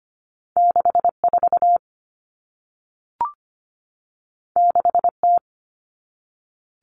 Morse code, heard as a keyed tone, about 25 wpm, 700 Hz. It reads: 64 6T
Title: Common Cut Numbers - 2-Digit 25wpm